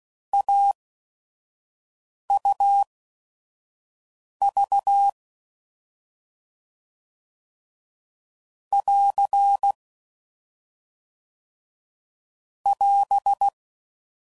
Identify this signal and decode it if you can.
Morse code, heard as a keyed tone, about 16 wpm, spaced as for 5 wpm, 800 Hz. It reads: AUV <AR> <AS>
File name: Combinations / Difficult Combos / AUV Short